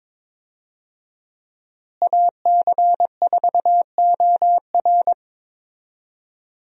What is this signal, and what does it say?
Morse code, heard as a keyed tone, about 22 wpm, 700 Hz. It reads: AC4OR
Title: USA Call Signs - 5 Characters - Repeated 3x - 22wpm